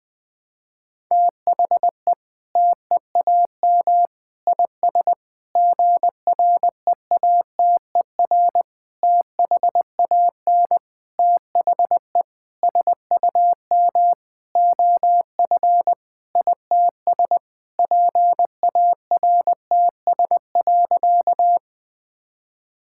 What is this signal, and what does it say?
Morse code, heard as a keyed tone, about 19 wpm, 700 Hz. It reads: THE TEAM IS GREATER THAN THE SUM OF ITS PARTS.
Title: Sentences with 8-Word Phrases - Spoken Beginning - 20wpm